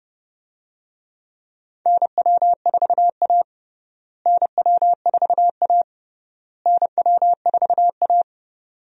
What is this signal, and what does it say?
Morse code, heard as a keyed tone, about 30 wpm, 700 Hz. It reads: NW4A NW4A NW4A
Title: USA Call Signs - 4 Characters - Repeated 3x - 30wpm